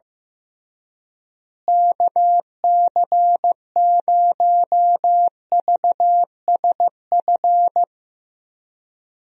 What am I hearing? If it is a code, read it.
Morse code, heard as a keyed tone, about 15 wpm, 700 Hz. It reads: KC0VSF
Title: USA Call Signs - 6 Characters - Repeated 2x - 15wpm